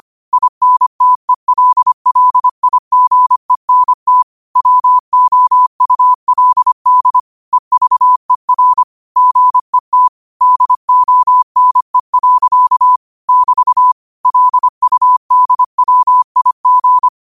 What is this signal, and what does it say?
Morse code, heard as a keyed tone, about 25 wpm, 1000 Hz. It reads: INTELLIGENT WOULD EVER GET DONE. <BT> LUDWIG
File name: qotdfiles / 25wpm